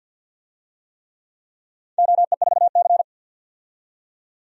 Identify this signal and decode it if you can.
Morse code, heard as a keyed tone, about 53 wpm, 700 Hz. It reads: OE4C
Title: DXCC-100 Call Signs - 4 Characters - 50wpm